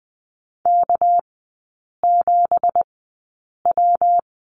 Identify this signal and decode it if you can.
Morse code, heard as a keyed tone, about 20 wpm, spaced as for 8 wpm, 690 Hz. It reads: K7W
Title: 198 Lesson 32e - Letter k - Three Characters 20wpm